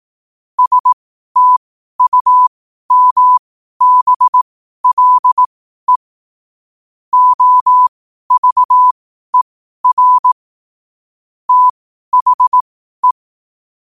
Morse code, heard as a keyed tone, about 18 wpm, spaced as for 12 wpm, 1000 Hz. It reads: STUMBLE OVER THE
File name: qotdfiles / 12wpm